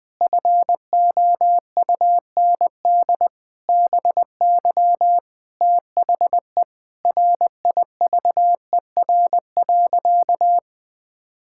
Morse code, heard as a keyed tone, about 20 wpm, 700 Hz. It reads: FOUND BY THE RIVER.
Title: Sentences from Top 1600 Words 20wpm